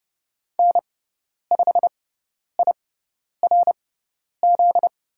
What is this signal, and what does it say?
Morse code, heard as a keyed tone, about 30 wpm, spaced as for 10 wpm, 700 Hz. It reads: N5IRZ
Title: USA Call Signs - 5 Characters - 6x Character Spacing - 30wpm